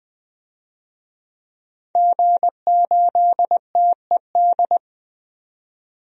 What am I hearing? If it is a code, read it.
Morse code, heard as a keyed tone, about 20 wpm, 700 Hz. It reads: G8TED
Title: DXCC-100 Call Signs - 5 Characters - 20wpm